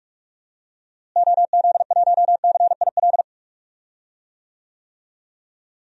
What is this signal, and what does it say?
Morse code, heard as a keyed tone, about 48 wpm, 700 Hz. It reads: OZ1CIL
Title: DXCC-100 Call Signs - 6 Characters - Repeated 2x - 45wpm